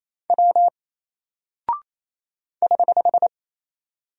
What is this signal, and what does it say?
Morse code, heard as a keyed tone, about 28 wpm, 700 Hz. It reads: W <HH>
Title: Single Character (Letter:Number:Punctuation:Prosign) Alt 28wpm